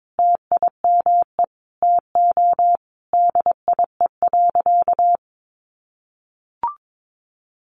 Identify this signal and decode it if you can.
Morse code, heard as a keyed tone, about 22 wpm, 700 Hz. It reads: TIME TO DIE.
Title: Sentences from Top 1300 Words 22wpm